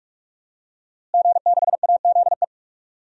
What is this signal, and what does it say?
Morse code, heard as a keyed tone, about 46 wpm, 700 Hz. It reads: G6AZE
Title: DXCC-100 Call Signs - 5 Characters - Repeated 2x - 45wpm